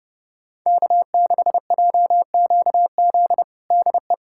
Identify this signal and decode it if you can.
Morse code, heard as a keyed tone, about 30 wpm, 700 Hz. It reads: K6JQZ DE
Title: USA Callsign DE Callsign - 5 Characters - 30wpm